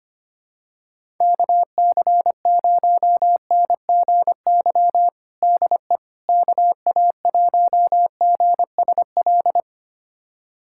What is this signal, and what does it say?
Morse code, heard as a keyed tone, about 25 wpm, 700 Hz. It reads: KC0NGY DE KA1GSL
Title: USA Callsign DE Callsign - 6 Characters - 25wpm